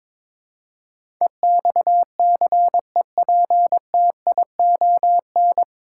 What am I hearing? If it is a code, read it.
Morse code, heard as a keyed tone, about 22 wpm, 700 Hz. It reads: EXCEPTION
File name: practice / 22wpm / Nine Letter Words - 22wpm